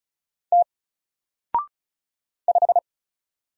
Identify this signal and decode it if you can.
Morse code, heard as a keyed tone, about 36 wpm, 700 Hz. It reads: T 5
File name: Single-Letter-Number-35wpm